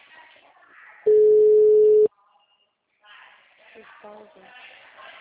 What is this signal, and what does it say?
Ringback tone (Europe, 425 Hz), with speech.
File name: out- -2010- - - .19317